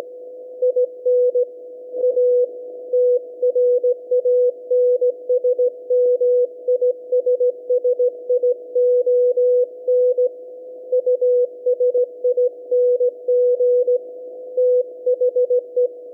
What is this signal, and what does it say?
Morse code, heard as a keyed tone, about 17 wpm, 500 Hz. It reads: IN A TRANSMISSION USING THE
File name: SAQ 17.20 cw